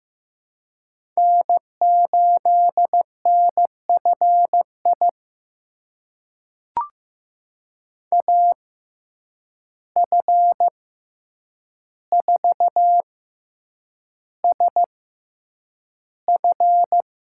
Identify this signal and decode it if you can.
Morse code, heard as a keyed tone, about 15 wpm, spaced as for 8 wpm, 700 Hz. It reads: N8NFI AF4SF
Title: USA Call Signs - 5 Characters - 6x Character Spacing - 15wpm